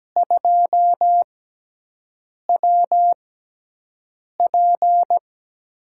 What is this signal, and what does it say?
Morse code, heard as a keyed tone, about 17 wpm, spaced as for 6 wpm, 700 Hz. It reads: 2WP
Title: USA Call Signs - 4 Characters - 6x Character Spacing - 17wpm